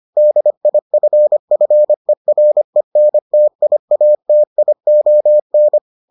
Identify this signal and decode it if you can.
Morse code, heard as a keyed tone, about 25 wpm, 600 Hz. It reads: DIFFERENTIATION